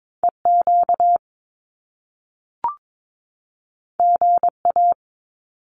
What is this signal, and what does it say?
Morse code, heard as a keyed tone, about 22 wpm, 700 Hz. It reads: EQ GA
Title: Top Two Letter Combinations in Words 22wpm